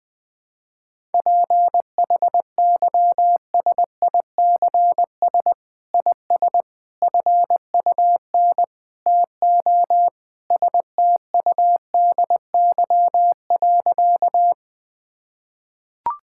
Morse code, heard as a keyed tone, about 21 wpm, 700 Hz. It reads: PHYSICS IS FUN TO STUDY.
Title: Sentences from Top 1000 Words - Review Part 5 - 20wpm